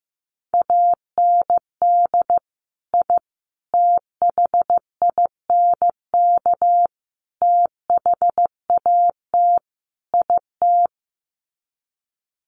Morse code, heard as a keyed tone, about 15 wpm, 700 Hz. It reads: AND I THINK THAT IT
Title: Sets of 5 Words Encore2 15wpm